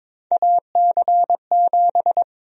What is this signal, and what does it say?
Morse code, heard as a keyed tone, about 22 wpm, 700 Hz. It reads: AC7
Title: USA Call Sign Prefixes - 22wpm